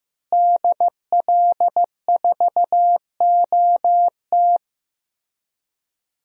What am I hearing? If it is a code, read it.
Morse code, heard as a keyed tone, about 15 wpm, 700 Hz. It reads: DL4OT
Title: USA Call Signs - 5 Characters - Repeated 3x - 15wpm